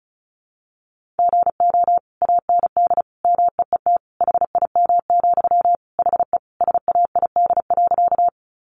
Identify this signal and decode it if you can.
Morse code, heard as a keyed tone, about 34 wpm, 700 Hz. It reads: GO AND MEET HIM, HE SAID.
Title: Sentences from Top 1100 Words 35wpm